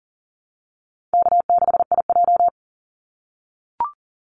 Morse code, heard as a keyed tone, about 40 wpm, 700 Hz. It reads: K6IJ
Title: Easy Call Signs - 40wpm